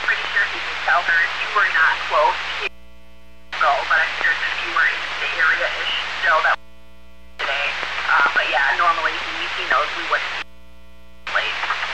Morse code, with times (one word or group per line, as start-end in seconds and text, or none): none